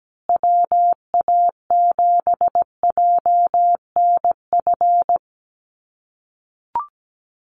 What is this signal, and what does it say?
Morse code, heard as a keyed tone, about 17 wpm, 700 Hz. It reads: WA7JNF
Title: USA Call Signs Repeated 3x 17wpm